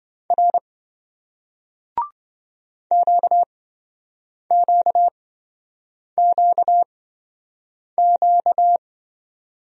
Morse code, heard as a keyed tone, about 27 wpm, spaced as for 7 wpm, 700 Hz. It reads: R QQQQ